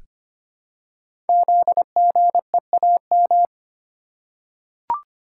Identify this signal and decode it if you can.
Morse code, heard as a keyed tone, about 25 wpm, 700 Hz. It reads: ZGEAM
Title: Code Groups - 5 Letters - 25wpm